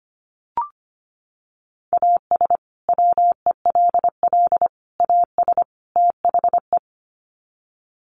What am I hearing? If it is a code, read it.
Morse code, heard as a keyed tone, about 25 wpm, 700 Hz. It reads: AS WELL AS THE
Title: Sets of 4 Words 25wpm